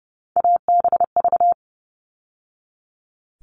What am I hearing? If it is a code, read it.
Morse code, heard as a keyed tone, about 30 wpm, 700 Hz. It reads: ABV
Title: All Cut Numbers - 3-Digit 30wpm